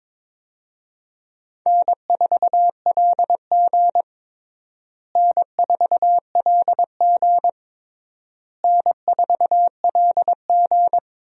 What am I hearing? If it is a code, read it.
Morse code, heard as a keyed tone, about 22 wpm, 700 Hz. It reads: N4LG N4LG N4LG
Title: USA Call Signs - 4 Characters - Repeated 3x - 22wpm